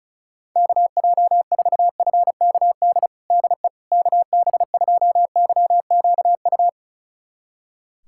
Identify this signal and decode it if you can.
Morse code, heard as a keyed tone, about 34 wpm, 700 Hz. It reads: KJ4FKD DE KB2YQU
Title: USA Callsign DE Callsign 35wpm